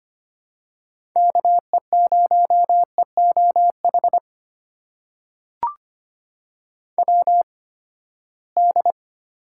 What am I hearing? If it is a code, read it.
Morse code, heard as a keyed tone, about 25 wpm, spaced as for 17 wpm, 700 Hz. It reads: KE0EOH WD
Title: USA Call Signs - 6 Characters - 8x Character Spacing - 25wpm